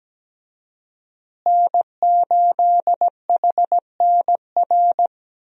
Morse code, heard as a keyed tone, about 17 wpm, 700 Hz. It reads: N8HNR
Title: USA Call Signs - 5 Characters - Repeated 3x - 17wpm